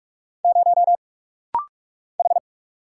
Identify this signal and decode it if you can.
Morse code, heard as a keyed tone, about 44 wpm, 700 Hz. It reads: 0 H
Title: Single Letter-Number - Rapid-Fire 45wpm